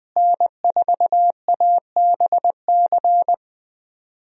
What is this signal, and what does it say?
Morse code, heard as a keyed tone, about 21 wpm, 700 Hz. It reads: N4ABC
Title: USA Call Signs - 5 Characters - Repeated 3x - 20wpm